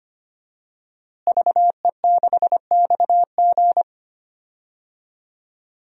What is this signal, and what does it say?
Morse code, heard as a keyed tone, about 25 wpm, 700 Hz. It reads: VE6XG
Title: DXCC-100 Call Signs - 5 Characters - Repeated 2x - 25wpm